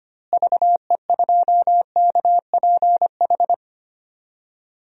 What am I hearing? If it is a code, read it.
Morse code, heard as a keyed tone, about 25 wpm, 700 Hz. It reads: VE2KPH
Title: DXCC-100 Call Signs - 6 Characters - Repeated 2x - 25wpm